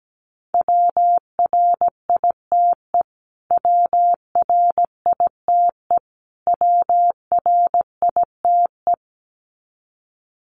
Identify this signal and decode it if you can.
Morse code, heard as a keyed tone, about 17 wpm, 700 Hz. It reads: WRITE WRITE WRITE
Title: Top 200 Words Repeated 17wpm